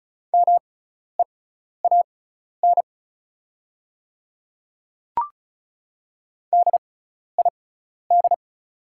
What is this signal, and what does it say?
Morse code, heard as a keyed tone, about 34 wpm, spaced as for 12 wpm, 700 Hz. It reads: MEAN DID
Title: 6x Character Spacing - Top 200 Words - 35wpm